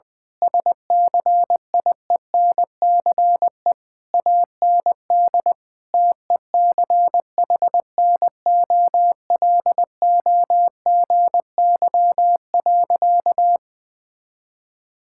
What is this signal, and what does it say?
Morse code, heard as a keyed tone, about 20 wpm, 700 Hz. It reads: SCIENCE AND TECHNOLOGY.
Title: Sentences from Top 1000 Words - Review Part 6 - 20wpm